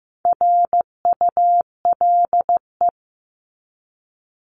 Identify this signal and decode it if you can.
Morse code, heard as a keyed tone, about 15 wpm, 700 Hz. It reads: RULE
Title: ES:S Suffix - Top 100 Words with Base Words 15wpm